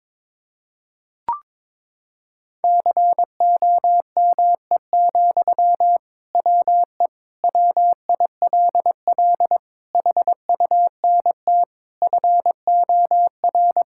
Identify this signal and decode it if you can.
Morse code, heard as a keyed tone, about 22 wpm, 700 Hz. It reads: COME, WE WILL HUNT FOR
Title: Sentences from Top 1600 Words 22wpm